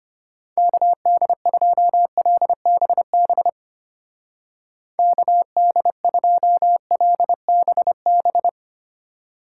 Morse code, heard as a keyed tone, about 29 wpm, 700 Hz. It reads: KD2LBB KD2LBB
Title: Speed Racing - USA 2x3 Call Signs 25wpm